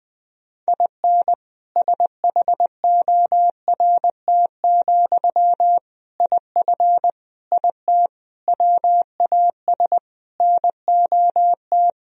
Morse code, heard as a keyed tone, about 20 wpm, 700 Hz. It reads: IN SHORT, IF IT WAS NOT
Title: Sentences from Top 1000 Words - Review Part 2 - 20wpm